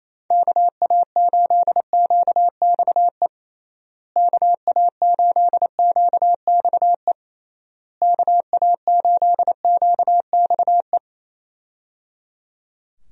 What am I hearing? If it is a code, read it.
Morse code, heard as a keyed tone, about 28 wpm, 700 Hz. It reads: KA8QXE KA8QXE KA8QXE
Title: USA Call Signs - 6 Characters - Repeated 3x - 28wpm